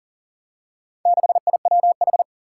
Morse code, heard as a keyed tone, about 40 wpm, 700 Hz. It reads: 6IWH
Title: Four Challenging Letter-Number Combinations 40wpm